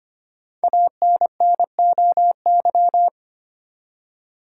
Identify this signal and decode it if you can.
Morse code, heard as a keyed tone, about 25 wpm, 700 Hz. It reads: ANNOY
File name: 5 Letter Words - First 3 Letters Given - 25wpm